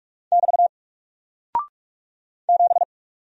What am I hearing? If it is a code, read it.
Morse code, heard as a keyed tone, about 48 wpm, 700 Hz. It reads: <BT> 7
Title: Single Character (Letter:Number:Punctuation:Prosign) Alt 45wpm